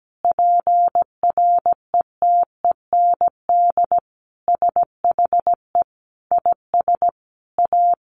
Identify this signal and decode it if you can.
Morse code, heard as a keyed tone, about 17 wpm, 700 Hz. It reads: PRETEND SHE IS A
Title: Additional Easy Phrases 17wpm